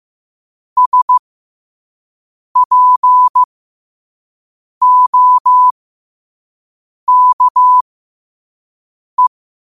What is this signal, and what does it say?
Morse code, heard as a keyed tone, about 15 wpm, spaced as for 5 wpm, 1000 Hz. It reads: SPOKE
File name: qotdfiles / 5wpm